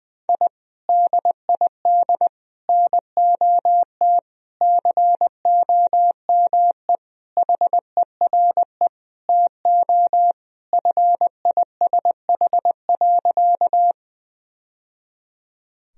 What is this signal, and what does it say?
Morse code, heard as a keyed tone, about 21 wpm, 700 Hz. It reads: I DID NOT COME HERE TO FISH.
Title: Sentences from Top 1000 Words - Review Part 1 - 20wpm